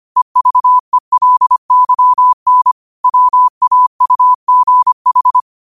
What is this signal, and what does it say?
Morse code, heard as a keyed tone, about 25 wpm, 1000 Hz. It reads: EVELYN WAUGH
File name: qotdfiles / 25wpm